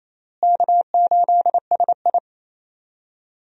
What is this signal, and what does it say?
Morse code, heard as a keyed tone, about 28 wpm, 700 Hz. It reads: K8SI
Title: USA Call Signs - 4 Characters - Repeated 3x - 28wpm